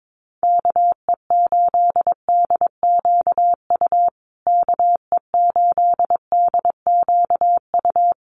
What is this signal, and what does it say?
Morse code, heard as a keyed tone, about 22 wpm, 700 Hz. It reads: KE8DQU KE8DQU
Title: USA Call Signs Repeated 3x 22wpm